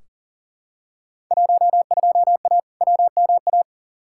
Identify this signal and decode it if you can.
Morse code, heard as a keyed tone, about 40 wpm, 700 Hz. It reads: 12A WMA
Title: Field Day Exchanges Encore - 40wpm